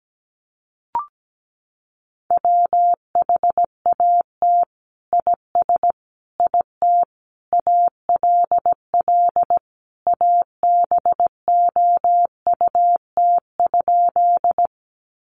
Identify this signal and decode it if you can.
Morse code, heard as a keyed tone, about 17 wpm, 700 Hz. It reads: WHAT IS IT ALL ABOUT?
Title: Sentences from Top 100 Words 17wpm